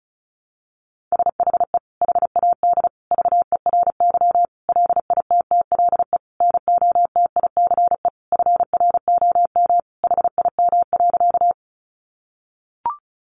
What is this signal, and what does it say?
Morse code, heard as a keyed tone, about 34 wpm, 700 Hz. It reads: SHE HAD VERY LITTLE NOTICE FROM HIM.
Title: Sentences from Top 400 Words 35wpm